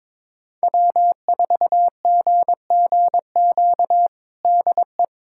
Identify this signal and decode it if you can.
Morse code, heard as a keyed tone, about 22 wpm, 700 Hz. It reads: W4GGQ DE
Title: USA Callsign DE Callsign - 5 Characters - 22wpm